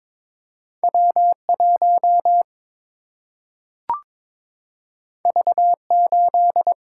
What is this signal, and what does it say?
Morse code, heard as a keyed tone, about 22 wpm, 700 Hz. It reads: W 1 V 8
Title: Single-Letter Single-Number 22wpm